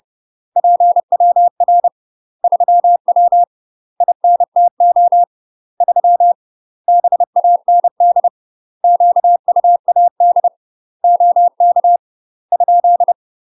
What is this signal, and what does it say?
Morse code, heard as a keyed tone, about 30 wpm, 700 Hz. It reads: PWR 3W INTO 3 BAND QUAD OK ?
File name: UK MASTER QSOs TRAINING / 30WPM / UK MASTER QSOs 0030